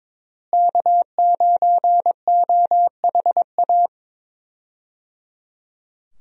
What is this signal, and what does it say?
Morse code, heard as a keyed tone, about 22 wpm, 700 Hz. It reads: K9OHA
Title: USA Call Signs - 5 Characters - Repeated 3x - 22wpm